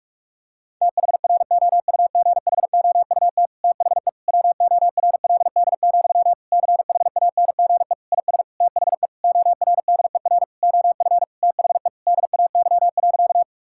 Morse code, heard as a keyed tone, about 46 wpm, 700 Hz. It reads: THROUGHOUT THE WORLD, CHANGE IS THE ORDER OF THE DAY.
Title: Sentences from Top 1000 Words - Review Part 5 - 45wpm